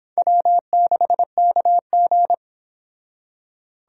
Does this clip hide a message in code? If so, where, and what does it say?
Yes: Morse code from 0.2 to 2.3 s, heard as a keyed tone, about 26 wpm, 700 Hz. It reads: W6KG